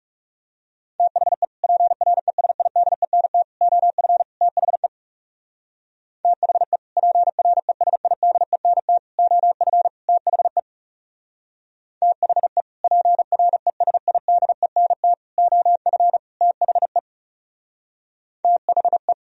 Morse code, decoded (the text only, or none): THE PRESIDENT OF THE THE PRESIDENT OF THE THE PRESIDENT OF THE THE